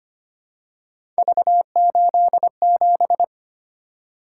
Morse code, heard as a keyed tone, about 25 wpm, 700 Hz. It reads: V87
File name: Three Letter-Number Combinations - 25wpm